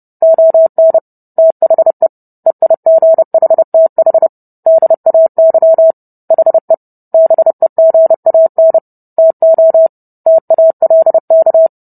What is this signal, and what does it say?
Morse code, heard as a keyed tone, about 30 wpm, 650 Hz. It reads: ON THE EIGHTH DAY HE BEGAN TO TALK